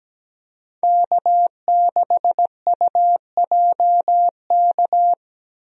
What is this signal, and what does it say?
Morse code, heard as a keyed tone, about 17 wpm, 700 Hz. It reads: K6UJK
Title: USA Call Signs - 5 Characters - Repeated 2x - 17wpm